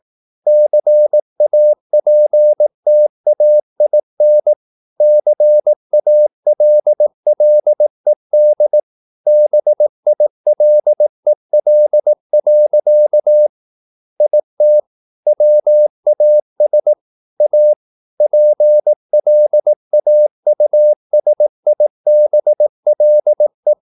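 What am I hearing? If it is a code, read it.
Morse code, heard as a keyed tone, about 18 wpm, 600 Hz. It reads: CAPTAIN CALLED BILEL. IT WAS A PLAUSIBLE
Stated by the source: Morse practice at 18 wpm, English text, from CW med Gnister.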